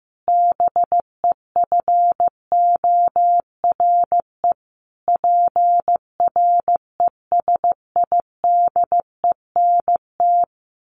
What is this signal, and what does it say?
Morse code, heard as a keyed tone, about 15 wpm, 700 Hz. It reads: BEFORE PRESIDENT
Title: A Sound Of Thunder 15wpm